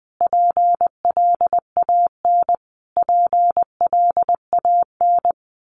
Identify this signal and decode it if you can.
Morse code, heard as a keyed tone, about 20 wpm, 700 Hz. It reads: PLAN PLAN
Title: Top 500 Words Repeated 20wpm